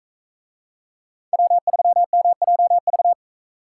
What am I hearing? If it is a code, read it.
Morse code, heard as a keyed tone, about 40 wpm, 700 Hz. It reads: W3MJV